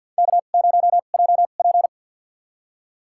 Morse code, heard as a keyed tone, about 50 wpm, spaced as for 36 wpm, 700 Hz. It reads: K0JP
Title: USA Call Signs - 4 Characters - 2x Character Spacing - 50wpm